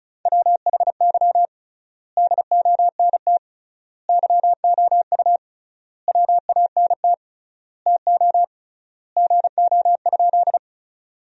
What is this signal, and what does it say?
Morse code, heard as a keyed tone, about 34 wpm, 700 Hz. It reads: WHY DONT YOU WANT TO GO?
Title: 3x Word-Spacing - Sentences from Top 300 Words - 35wpm